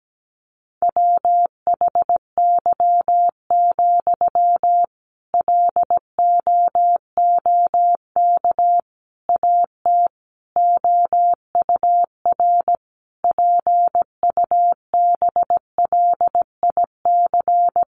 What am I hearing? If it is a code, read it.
Morse code, heard as a keyed tone, about 17 wpm, 700 Hz. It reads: WHY, LOOK AT OUR PUBLIC
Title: Sentences from Top 600 Words 17wpm